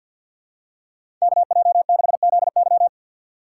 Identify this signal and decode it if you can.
Morse code, heard as a keyed tone, about 53 wpm, 700 Hz. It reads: KJ6ZY
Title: USA Call Signs - 5 Characters - 3x Character Spacing - 50wpm